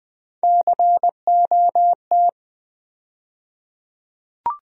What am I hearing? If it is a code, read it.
Morse code, heard as a keyed tone, about 20 wpm, 700 Hz. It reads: COT
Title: USA Call Sign Postfixes - 3 Letters 20wpm